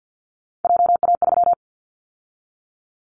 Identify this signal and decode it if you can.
Morse code, heard as a keyed tone, about 50 wpm, 700 Hz. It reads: JA3
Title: Callsign Prefixes for Top 50 DXCC - 50wpm